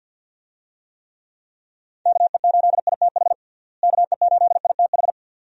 Morse code, heard as a keyed tone, about 50 wpm, 700 Hz. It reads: KE8ITH KE8ITH
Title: USA Call Signs - 6 Characters - Repeated 2x - 50wpm